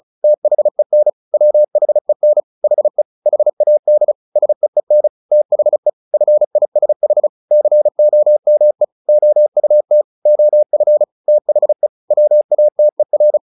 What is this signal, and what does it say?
Morse code, heard as a keyed tone, about 34 wpm, 600 Hz. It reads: THEN WHEN HE HAD SEEN THE FISH COME OUT OF THE WATER
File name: the old man and the sea 2 35wpm